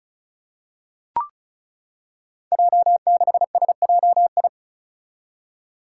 Morse code, heard as a keyed tone, about 36 wpm, 700 Hz. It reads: J6SJI
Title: Five Challenging Letter-Number Combinations 35wpm